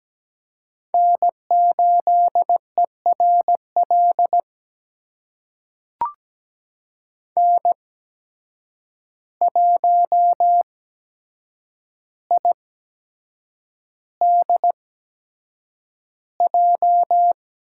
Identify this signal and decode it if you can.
Morse code, heard as a keyed tone, about 17 wpm, spaced as for 7 wpm, 700 Hz. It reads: N8ERL N1IDJ
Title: USA Call Signs - 8x Character Spacing - 17wpm